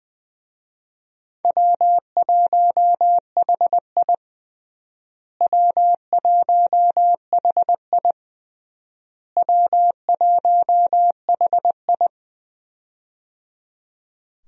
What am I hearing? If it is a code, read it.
Morse code, heard as a keyed tone, about 20 wpm, 700 Hz. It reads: W1HI W1HI W1HI
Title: USA Call Signs - 4 Characters - Repeated 3x - 20wpm